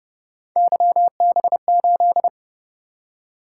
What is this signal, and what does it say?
Morse code, heard as a keyed tone, about 29 wpm, 700 Hz. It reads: YB8